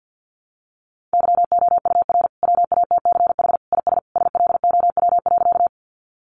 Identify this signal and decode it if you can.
Morse code, heard as a keyed tone, about 50 wpm, 700 Hz. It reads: YOUR WATCH IS SLOW.
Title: Sentences from Top 400 Words 50wpm